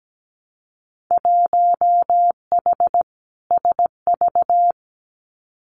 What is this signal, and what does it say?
Morse code, heard as a keyed tone, about 17 wpm, 700 Hz. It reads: 1H SV
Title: Winter Field Day Exchanges 17wpm